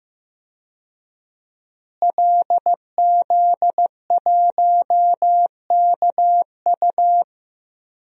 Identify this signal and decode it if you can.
Morse code, heard as a keyed tone, about 15 wpm, 700 Hz. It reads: LZ1KU
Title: DXCC-100 Call Signs - 5 Characters - Repeated 3x - 15wpm